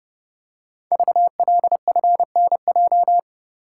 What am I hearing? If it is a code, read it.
Morse code, heard as a keyed tone, about 29 wpm, 700 Hz. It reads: VLFNJ
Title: Code Groups - 5 Letters - 30wpm